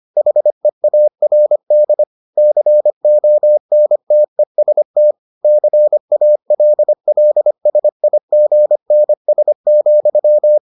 Morse code, heard as a keyed tone, about 25 wpm, 600 Hz. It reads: HEARD CONTEST CALLSIGNS,